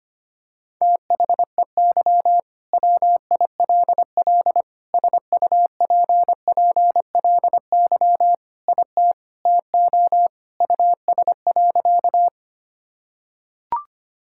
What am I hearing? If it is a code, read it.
Morse code, heard as a keyed tone, about 25 wpm, 700 Hz. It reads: THEY WILL SUPPLY IT TO US.
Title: Sentences from Top 1000 Words - Review Part 5 - 25wpm